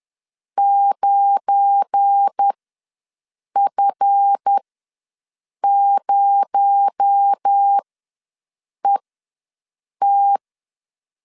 Morse code, heard as a keyed tone, about 10 wpm, spaced as for 6 wpm, 800 Hz. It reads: 9F0ET